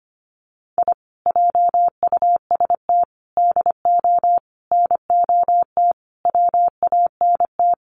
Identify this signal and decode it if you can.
Morse code, heard as a keyed tone, about 25 wpm, 700 Hz. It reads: I JUST DO NOT WANT
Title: Sets of 5 Words Encore2 25wpm